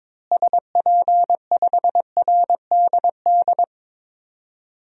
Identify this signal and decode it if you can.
Morse code, heard as a keyed tone, about 21 wpm, 700 Hz. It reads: SP5RDD
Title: DXCC-100 Call Signs - 6 Characters - Repeated 2x - 22wpm